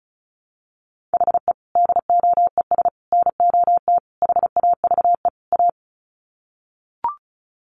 Morse code, heard as a keyed tone, about 34 wpm, 700 Hz. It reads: HE DOES NOT HAVE A
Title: Sets of 5 Words Encore 35wpm